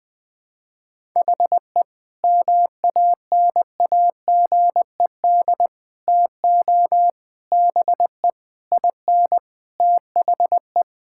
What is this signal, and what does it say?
Morse code, heard as a keyed tone, about 20 wpm, 700 Hz. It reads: HE MANAGED TO BE IN THE
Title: Sentences with 9-Word Phrases - 20wpm